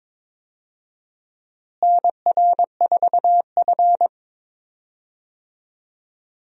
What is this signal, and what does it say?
Morse code, heard as a keyed tone, about 21 wpm, 700 Hz. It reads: NR4F
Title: USA Call Signs - 4 Characters - Repeated 2x - 22wpm